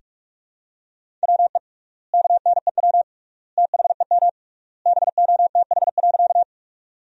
Morse code, heard as a keyed tone, about 46 wpm, 700 Hz. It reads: WE KNEW THEM BOTH.
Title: 3x Word-Spacing - Sentences from Top 400 Words - 45wpm